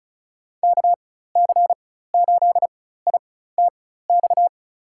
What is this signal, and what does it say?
Morse code, heard as a keyed tone, about 34 wpm, spaced as for 16 wpm, 700 Hz. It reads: KC8ITX
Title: USA Call Signs - 6 Characters - 4x Character Spacing - 35wpm